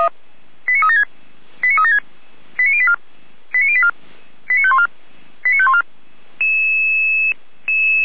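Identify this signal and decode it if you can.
Five-tone selective call (ZVEI), heard as quick runs of five tones, tones of about 70 ms. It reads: 89376 89376 89084 89084 89514 89514
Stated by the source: ZVEI five-tone selective call received with an SDR.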